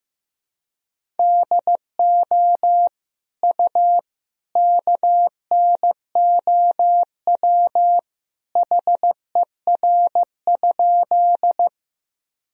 Morse code, heard as a keyed tone, about 15 wpm, 700 Hz. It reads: DO U KNOW HER?
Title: Sentences from Top 100 Words and Abbreviations 15wpm